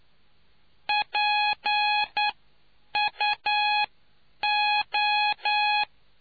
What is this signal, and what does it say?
Morse code, heard as a keyed tone, about 9 wpm, spaced as for 8 wpm, 840 Hz. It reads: PUO